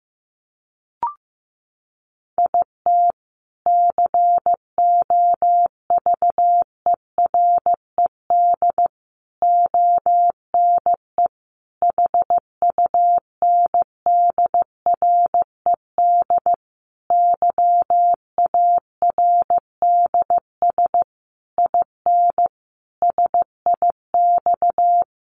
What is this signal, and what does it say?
Morse code, heard as a keyed tone, about 15 wpm, 700 Hz. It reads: IT COVERED ONE HUNDRED YARDS IN SIX
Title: A Sound Of Thunder 15wpm